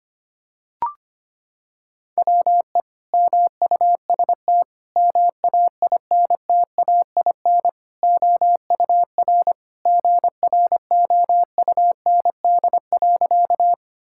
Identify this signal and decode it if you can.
Morse code, heard as a keyed tone, about 25 wpm, 700 Hz. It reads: WE MUST MAINTAIN OUR GROUND.
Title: Sentences from Top 1000 Words - Review Part 5 - 25wpm